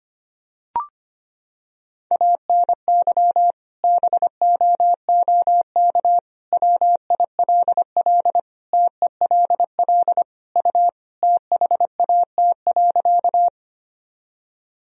Morse code, heard as a keyed tone, about 25 wpm, 700 Hz. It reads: ANY BOOK WILL TELL U THAT.
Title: Sentences from Top 300 Words and Abbreviations 25wpm